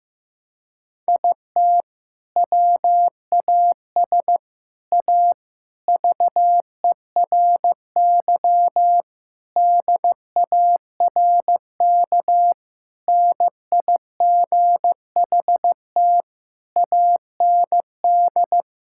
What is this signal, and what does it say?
Morse code, heard as a keyed tone, about 16 wpm, 700 Hz. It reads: IT WAS A VERY DARK NIGHT AND
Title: Sentences from Top 1600 Words 15wpm